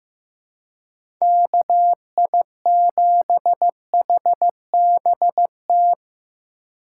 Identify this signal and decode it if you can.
Morse code, heard as a keyed tone, about 15 wpm, 700 Hz. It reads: KI7HBT
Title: USA Call Signs - 6 Characters - 4x Character Spacing - 15wpm